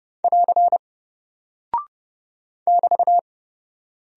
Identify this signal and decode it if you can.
Morse code, heard as a keyed tone, about 30 wpm, 700 Hz. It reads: <AR> <BT>
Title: Single Character (Letter:Number:Punctuation:Prosign) 30wpm